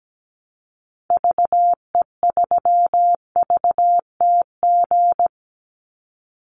Morse code, heard as a keyed tone, about 17 wpm, 700 Hz. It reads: VE3VTG
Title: Call Signs - 17wpm